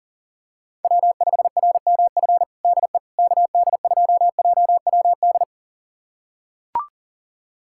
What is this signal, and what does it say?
Morse code, heard as a keyed tone, about 40 wpm, 700 Hz. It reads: W5RMF DE KD2JWD
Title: USA Callsign DE Callsign 40wpm